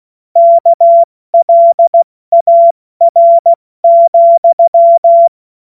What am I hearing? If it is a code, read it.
Morse code, heard as a keyed tone, about 16 wpm, 680 Hz. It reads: KLAR,
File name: Morsekurs / Witze / Witz 22. August 2023 70 F80